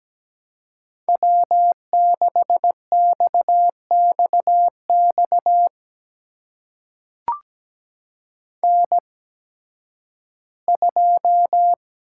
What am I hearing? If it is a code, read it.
Morse code, heard as a keyed tone, about 17 wpm, spaced as for 11 wpm, 700 Hz. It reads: W6XXX N2
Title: USA Call Signs - 5 Characters - 8x Character Spacing - 17wpm